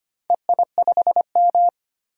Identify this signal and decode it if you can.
Morse code, heard as a keyed tone, about 25 wpm, 700 Hz. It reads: EI5M
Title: DXCC-100 Call Signs - 4 Characters - 25wpm